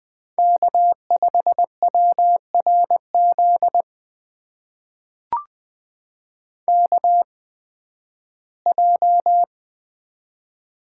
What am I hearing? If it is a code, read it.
Morse code, heard as a keyed tone, about 20 wpm, spaced as for 13 wpm, 700 Hz. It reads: K5WRZ KJ
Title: USA Call Signs - 8x Character Spacing - 20wpm